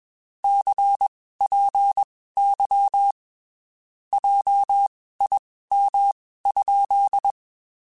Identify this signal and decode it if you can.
Morse code, heard as a keyed tone, about 21 wpm, spaced as for 15 wpm, 800 Hz. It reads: CPY JIM?
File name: QSOs / QSOs-mp3s / WD0BGP 1@20